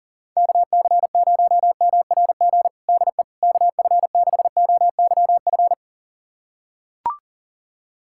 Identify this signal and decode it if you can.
Morse code, heard as a keyed tone, about 40 wpm, 700 Hz. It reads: KC0MRG DE KF6OYF
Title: USA Callsign DE Callsign - 6 Characters - 40wpm